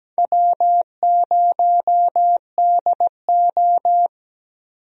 Morse code, heard as a keyed tone, about 17 wpm, 700 Hz. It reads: W0DO
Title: USA Call Signs - 4 Characters - 17wpm